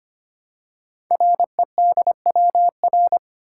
Morse code, heard as a keyed tone, about 25 wpm, 700 Hz. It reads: REDWR
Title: Code Groups - 5 Letters - 25wpm